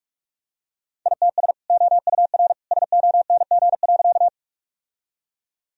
Morse code, heard as a keyed tone, about 46 wpm, 700 Hz. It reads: ITS OUR SONG.
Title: Sentences from Top 1000 Words - Review Part 2 - 45wpm